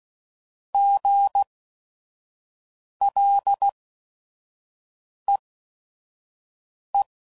Morse code, heard as a keyed tone, about 16 wpm, spaced as for 5 wpm, 800 Hz. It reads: GLEE
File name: Prefixes / By First Letter / Gs